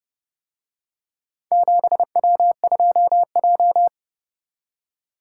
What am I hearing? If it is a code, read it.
Morse code, heard as a keyed tone, about 28 wpm, 700 Hz. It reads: 7W2J